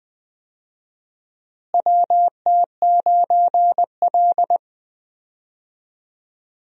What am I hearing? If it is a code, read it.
Morse code, heard as a keyed tone, about 20 wpm, 700 Hz. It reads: WT9L
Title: USA Call Signs - 4 Characters - Repeated 3x - 20wpm